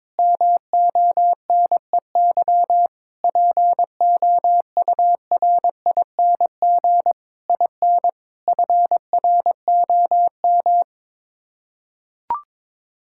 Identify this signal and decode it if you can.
Morse code, heard as a keyed tone, about 21 wpm, 700 Hz. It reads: MONEY POURING IN FROM
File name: Sets of 9 Words - 22wpm